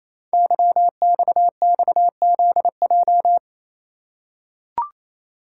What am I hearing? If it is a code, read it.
Morse code, heard as a keyed tone, about 28 wpm, 700 Hz. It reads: YXXZJ
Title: Code Groups - 5 Letters - 28wpm